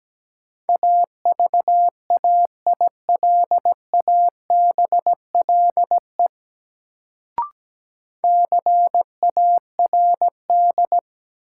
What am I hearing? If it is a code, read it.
Morse code, heard as a keyed tone, about 17 wpm, 700 Hz. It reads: AVAILABLE CARD
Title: Sentences from Top 600 Words - New Word Drill - 17wpm